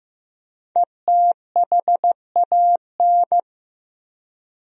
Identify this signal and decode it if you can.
Morse code, heard as a keyed tone, about 15 wpm, 700 Hz. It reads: ETHAN
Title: USA Common Names - 15wpm